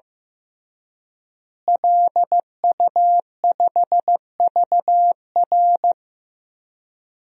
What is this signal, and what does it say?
Morse code, heard as a keyed tone, about 15 wpm, 700 Hz. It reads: LU5VR
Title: DXCC-100 Call Signs - 5 Characters - Repeated 3x - 15wpm